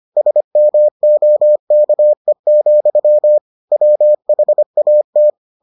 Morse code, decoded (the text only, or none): SMOKE, WHAT